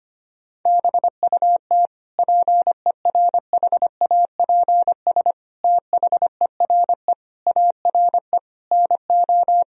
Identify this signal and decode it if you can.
Morse code, heard as a keyed tone, about 25 wpm, 700 Hz. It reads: BUT PERHAPS THERE ARE NO
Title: Sentences from Top 1000 Words - Review Part 3 - 25wpm